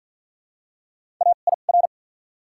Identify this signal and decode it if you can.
Morse code, heard as a keyed tone, about 49 wpm, spaced as for 35 wpm, 700 Hz. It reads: AIR
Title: 2x Character Spacing - Top 500 Words - 50wpm